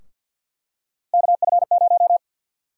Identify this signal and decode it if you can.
Morse code, heard as a keyed tone, about 49 wpm, 700 Hz. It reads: KF0
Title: USA Call Sign Prefixes - 50wpm